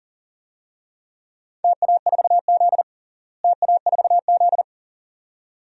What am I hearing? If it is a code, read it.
Morse code, heard as a keyed tone, about 40 wpm, 700 Hz. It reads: TA4Z TA4Z
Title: DXCC-100 Call Signs - 4 Characters - Repeated 2x - 40wpm